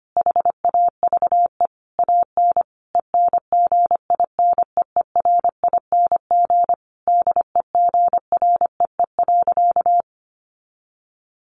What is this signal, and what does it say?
Morse code, heard as a keyed tone, about 25 wpm, 700 Hz. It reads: HAVE AN ENGINEERING DEGREE.
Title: Sentences from Top 1500 Words 25wpm